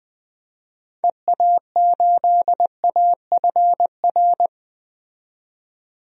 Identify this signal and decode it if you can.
Morse code, heard as a keyed tone, about 20 wpm, 700 Hz. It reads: EA8AFR
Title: DXCC-100 Call Signs - 6 Characters - 20wpm